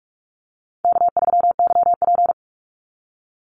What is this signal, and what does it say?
Morse code, heard as a keyed tone, about 44 wpm, 700 Hz. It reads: K3YP
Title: Easy Call Signs - 45wpm